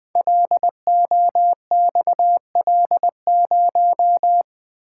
Morse code, heard as a keyed tone, about 20 wpm, 700 Hz. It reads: LOXL0
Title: Code Groups - 5 Letter:Numbers - 20wpm